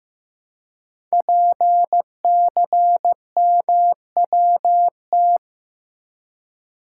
Morse code, heard as a keyed tone, about 15 wpm, 700 Hz. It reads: PCMWT
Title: Code Groups - 5 Letters - 15wpm